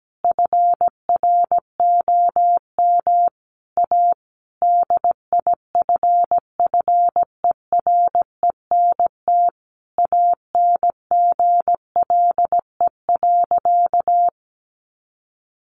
Morse code, heard as a keyed tone, about 17 wpm, 700 Hz. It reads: FROM A DIFFERENT ANGLE.
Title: Sentences from Top 1500 Words 17wpm